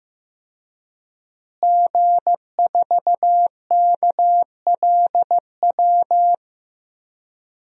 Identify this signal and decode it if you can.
Morse code, heard as a keyed tone, about 15 wpm, 700 Hz. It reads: G4KLW
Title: DXCC-100 Call Signs - 5 Characters - Repeated 3x - 15wpm